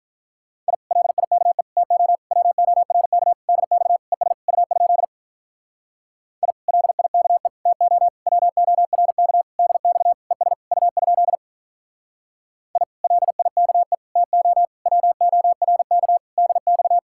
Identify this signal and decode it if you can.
Morse code, heard as a keyed tone, about 46 wpm, 700 Hz. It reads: I LIKE TO WORK DX ES U? I LIKE TO WORK DX ES U? I LIKE TO WORK DX